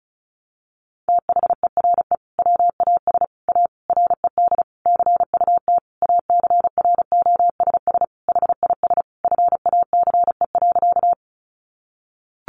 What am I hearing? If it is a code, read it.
Morse code, heard as a keyed tone, about 34 wpm, 700 Hz. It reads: THERE WAS A RED CUT ACROSS HIS FACE.
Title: Sentences from Top 700 Words 35wpm